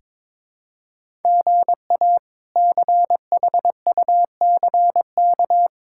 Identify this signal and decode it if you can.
Morse code, heard as a keyed tone, about 22 wpm, 700 Hz. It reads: GA CHUCK
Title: K1USN SST QSO Practice - 22wpm